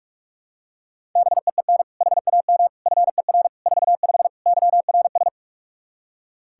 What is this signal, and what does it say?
Morse code, heard as a keyed tone, about 42 wpm, 700 Hz. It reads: BEEN HAM FER 45 YRS
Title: Practice QSO - Exchange 3 - More Info - 45wpm